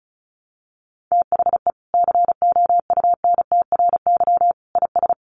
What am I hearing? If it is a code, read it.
Morse code, heard as a keyed tone, about 34 wpm, 700 Hz. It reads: THE COUNTRY IS
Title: Sentences from Top 1100 Words 35wpm